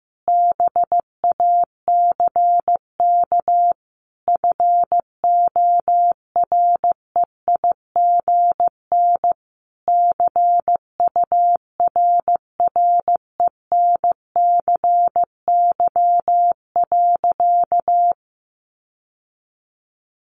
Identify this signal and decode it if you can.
Morse code, heard as a keyed tone, about 15 wpm, 700 Hz. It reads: BACK FOREIGN CURRENCY.
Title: Sentences from Top 1500 Words 15wpm